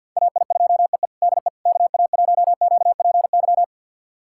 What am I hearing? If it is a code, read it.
Morse code, heard as a keyed tone, about 47 wpm, 700 Hz. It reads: AI2EE DE KA1QPY
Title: USA Callsign DE Callsign 50wpm